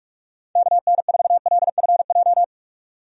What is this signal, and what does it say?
Morse code, heard as a keyed tone, about 46 wpm, 700 Hz. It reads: KN4LFJ
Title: USA Call Signs - 6 Characters - 45wpm